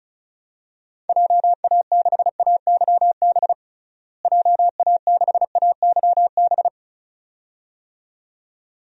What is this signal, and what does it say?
Morse code, heard as a keyed tone, about 36 wpm, 700 Hz. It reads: JA6AYB JA6AYB
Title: DXCC-100 Call Signs - 6 Characters - Repeated 2x - 35wpm